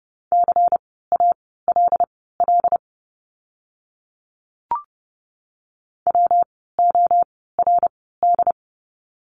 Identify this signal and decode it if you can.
Morse code, heard as a keyed tone, about 30 wpm, spaced as for 17 wpm, 700 Hz. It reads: CALL WORD
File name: course / 30:10wpm / 115 Lesson 19f - Letter w - Words 30wpm